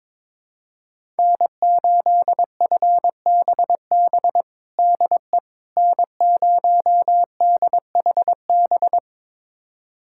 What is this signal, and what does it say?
Morse code, heard as a keyed tone, about 22 wpm, 700 Hz. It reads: N8FBB DE N0DHB
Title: USA Callsign DE Callsign - 5 Characters - 22wpm